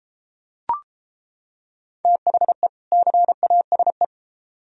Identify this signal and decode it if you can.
Morse code, heard as a keyed tone, about 34 wpm, 700 Hz. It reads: THE CASE